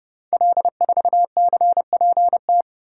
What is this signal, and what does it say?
Morse code, heard as a keyed tone, about 30 wpm, 700 Hz. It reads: L4CPT
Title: Code Groups - 5 Letter:Numbers - 30wpm